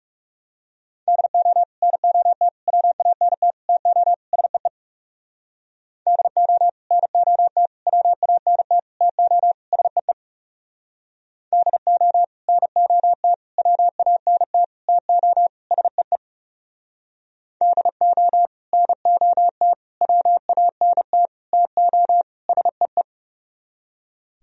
Morse code, heard as a keyed tone, about 38 wpm, 700 Hz. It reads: DO NOT WANT TO SEE DO NOT WANT TO SEE DO NOT WANT TO SEE DO NOT WANT TO SEE